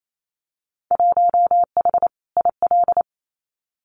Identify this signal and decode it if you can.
Morse code, heard as a keyed tone, about 28 wpm, 700 Hz. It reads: 1H IL
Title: Winter Field Day Exchanges 28wpm